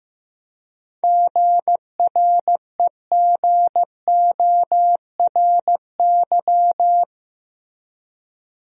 Morse code, heard as a keyed tone, about 16 wpm, 700 Hz. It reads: GREGORY
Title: USA Common Names - 15wpm